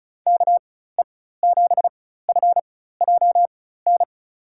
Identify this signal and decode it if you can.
Morse code, heard as a keyed tone, about 36 wpm, spaced as for 16 wpm, 700 Hz. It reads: KE7FJN
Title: USA Call Signs - 6 Characters - 4x Character Spacing - 35wpm